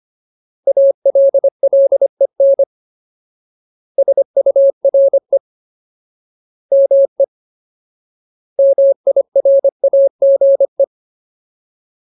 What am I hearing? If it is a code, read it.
Morse code, heard as a keyed tone, about 25 wpm, 550 Hz. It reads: ALLEN SURE ME MIRAGE
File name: Ultimate Word List / 25WPM Ultimate